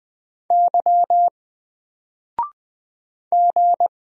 Morse code, heard as a keyed tone, about 20 wpm, 700 Hz. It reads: Y G
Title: Single Character (Letter:Number:Punctuation:Prosign) Alt 20wpm